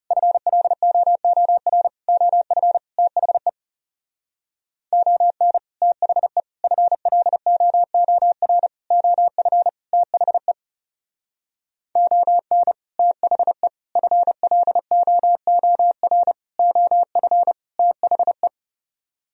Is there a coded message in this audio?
Yes: Morse code from 0.1 to 18.5 s, heard as a keyed tone, about 39 wpm, 700 Hz. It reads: FLOOR OF THE ON THE FLOOR OF THE ON THE FLOOR OF THE